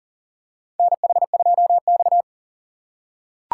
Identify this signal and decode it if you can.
Morse code, heard as a keyed tone, about 42 wpm, 700 Hz. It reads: NH2X
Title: USA Call Signs - 4 Characters - 2x Character Spacing - 40wpm